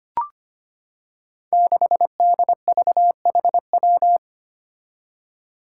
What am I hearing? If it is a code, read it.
Morse code, heard as a keyed tone, about 25 wpm, 700 Hz. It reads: 6DVHW
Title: Five Challenging Letter-Number Combinations 25wpm